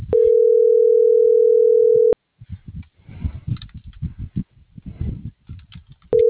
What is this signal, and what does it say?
Ringback tone (North America), 440 + 480 Hz.